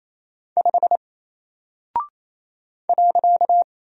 Morse code, heard as a keyed tone, about 27 wpm, 700 Hz. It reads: 5 .
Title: Single Character (Letter:Number:Punctuation:Prosign) Alt 28wpm